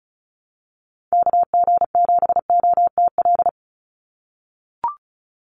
Morse code, heard as a keyed tone, about 34 wpm, 700 Hz. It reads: KG7OTL
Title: USA Call Signs 35wpm